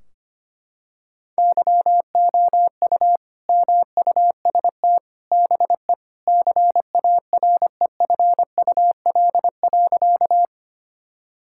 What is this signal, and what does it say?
Morse code, heard as a keyed tone, about 25 wpm, 700 Hz. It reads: YOU MUST BE CAREFUL.
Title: Sentences from Top 1600 Words 25wpm